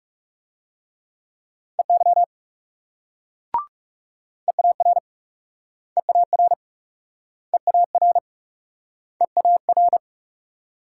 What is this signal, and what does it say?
Morse code, heard as a keyed tone, about 38 wpm, 700 Hz. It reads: EY EAR EAR EAR EAR